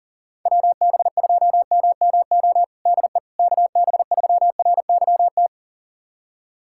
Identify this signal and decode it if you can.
Morse code, heard as a keyed tone, about 40 wpm, 700 Hz. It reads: WB2MMO DE KB3RYT
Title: USA Callsign DE Callsign - 6 Characters - 40wpm